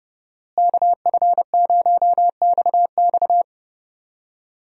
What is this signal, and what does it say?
Morse code, heard as a keyed tone, about 30 wpm, 700 Hz. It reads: KF0XX
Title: USA Call Signs - 5 Characters - Repeated 2x - 30wpm